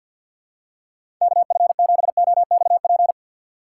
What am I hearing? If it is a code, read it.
Morse code, heard as a keyed tone, about 52 wpm, 700 Hz. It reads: KF7OXP
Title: USA Call Signs - 3x Character Spacing - 50wpm